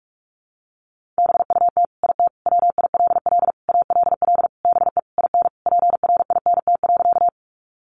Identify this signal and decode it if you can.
Morse code, heard as a keyed tone, about 44 wpm, 700 Hz. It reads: BUT IT WILL ALL BE IN PRINT.
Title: Sentences from Top 1400 Words 45wpm